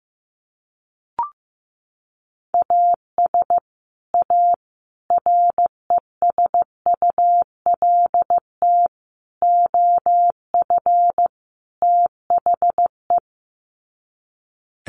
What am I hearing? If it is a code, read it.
Morse code, heard as a keyed tone, about 15 wpm, 700 Hz. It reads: AS A RESULT OF THE
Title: Sets of 5 Words 15wpm